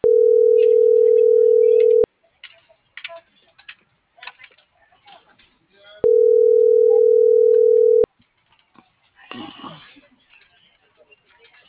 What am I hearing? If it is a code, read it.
Ringback tone (North America), 440 + 480 Hz.